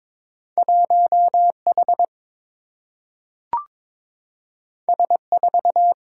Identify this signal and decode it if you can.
Morse code, heard as a keyed tone, about 22 wpm, 700 Hz. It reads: 1H S4
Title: Single-Letter Single-Number 22wpm